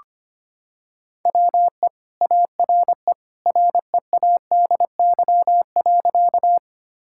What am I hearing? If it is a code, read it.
Morse code, heard as a keyed tone, about 25 wpm, 700 Hz. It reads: WE ARE READY.
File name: practice / 25wpm / Sentences with 2-Word Phrases - 25wpm